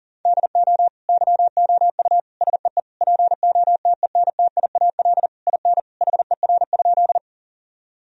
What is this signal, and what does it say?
Morse code, heard as a keyed tone, about 40 wpm, 700 Hz. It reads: DO YOU SEE POTENTIAL IN HER?
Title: Sentences from Top 1000 Words - Review Part 4 - 40wpm